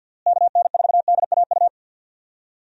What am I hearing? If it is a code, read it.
Morse code, heard as a keyed tone, about 52 wpm, 700 Hz. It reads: KN4DAU
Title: USA Call Signs - 6 Characters - Repeated 2x - 50wpm